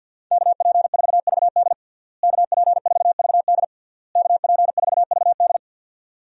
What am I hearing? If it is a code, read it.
Morse code, heard as a keyed tone, about 51 wpm, 700 Hz. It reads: KP4VD KP4VD KP4VD
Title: DXCC-100 Call Signs - 5 Characters - Repeated 3x - 50wpm